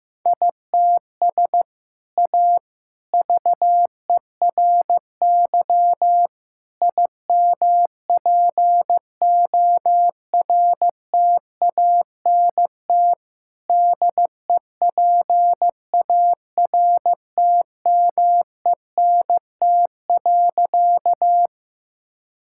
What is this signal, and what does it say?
Morse code, heard as a keyed tone, about 15 wpm, 700 Hz. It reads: ITS A VERY IMPORTANT DEPARTMENT.
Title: Sentences from Top 1000 Words - Review Part 5 - 15wpm